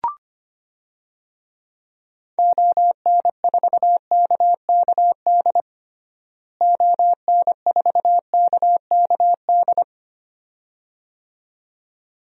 Morse code, heard as a keyed tone, about 25 wpm, 700 Hz. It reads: ON4KKD ON4KKD
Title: DXCC-100 Call Signs - 6 Characters - Repeated 2x - 25wpm